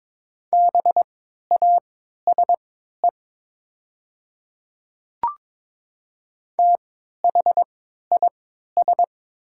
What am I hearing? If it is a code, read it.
Morse code, heard as a keyed tone, about 22 wpm, spaced as for 12 wpm, 690 Hz. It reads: BASE THIS
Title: 3x Character Spacing - Top 300 Words - 22wpm